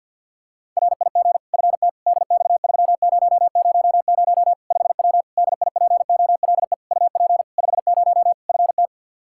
Morse code, heard as a keyed tone, about 53 wpm, 700 Hz. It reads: RIG FT DX3000 5W DIPOLE UP 50 FT
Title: Practice QSO - Exchange 2 - Info - 50wpm